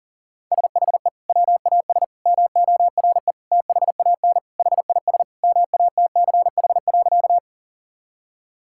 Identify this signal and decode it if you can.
Morse code, heard as a keyed tone, about 40 wpm, 700 Hz. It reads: SHE WAS MORE THAN HIS MATCH.
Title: Sentences from Top 1600 Words 40wpm